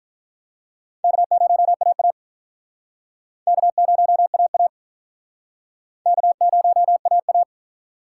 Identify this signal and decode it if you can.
Morse code, heard as a keyed tone, about 46 wpm, 700 Hz. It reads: K0AA K0AA K0AA